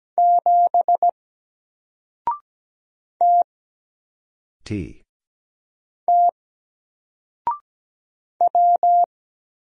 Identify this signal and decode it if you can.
Morse code, heard as a keyed tone, about 17 wpm, 700 Hz. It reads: 7 T T W
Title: Single Character (Letter:Number:Punctuation:Prosign) 17wpm